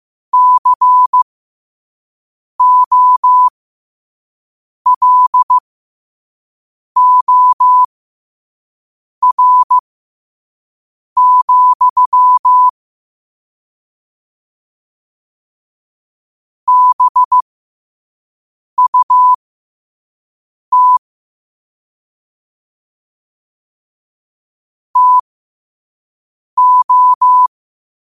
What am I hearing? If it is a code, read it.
Morse code, heard as a keyed tone, about 15 wpm, spaced as for 5 wpm, 1000 Hz. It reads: COLOR, BUT TO
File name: qotdfiles / 5wpm